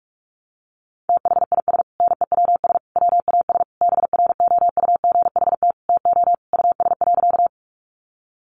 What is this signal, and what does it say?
Morse code, heard as a keyed tone, about 44 wpm, 700 Hz. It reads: THIS NEWS WAS BROUGHT TO US.
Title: Sentences from Top 800 Words 45wpm